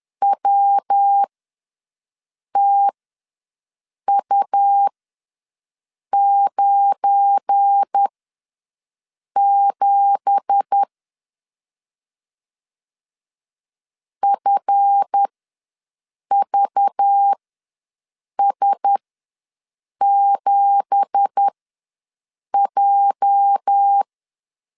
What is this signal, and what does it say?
Morse code, heard as a keyed tone, about 10 wpm, spaced as for 6 wpm, 800 Hz. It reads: WTU97 FVS7J
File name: LEZIONI / 9 2867 ws 10wpm 4215